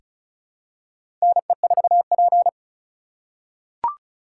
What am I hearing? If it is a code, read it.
Morse code, heard as a keyed tone, about 36 wpm, 700 Hz. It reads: NE4P
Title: USA Call Signs - 4 Characters - 4x Character Spacing - 35wpm